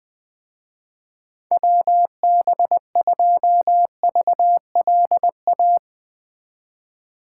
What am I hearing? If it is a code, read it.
Morse code, heard as a keyed tone, about 20 wpm, 700 Hz. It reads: WB2VLA